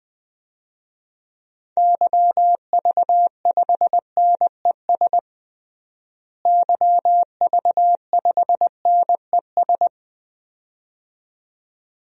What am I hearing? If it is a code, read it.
Morse code, heard as a keyed tone, about 20 wpm, 700 Hz. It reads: YV5NES YV5NES
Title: DXCC-100 Call Signs - 6 Characters - Repeated 2x - 20wpm